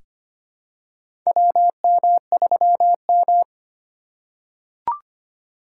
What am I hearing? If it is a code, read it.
Morse code, heard as a keyed tone, about 25 wpm, 700 Hz. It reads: WM3M
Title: USA Call Signs - 4 Characters - 3x Character Spacing - 25wpm